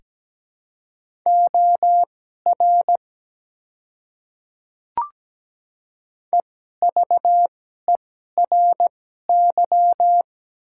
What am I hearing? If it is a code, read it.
Morse code, heard as a keyed tone, about 17 wpm, spaced as for 12 wpm, 700 Hz. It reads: OR EVERY
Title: 2x Character Spacing - Top 200 Words - 17wpm